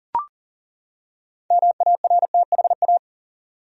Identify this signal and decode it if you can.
Morse code, heard as a keyed tone, about 40 wpm, 700 Hz. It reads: MARTHA
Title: USA Common Names - 40wpm